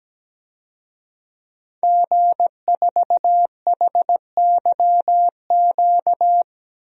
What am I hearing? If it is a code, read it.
Morse code, heard as a keyed tone, about 17 wpm, 700 Hz. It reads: G4HYQ
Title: DXCC-100 Call Signs - 5 Characters - Repeated 2x - 17wpm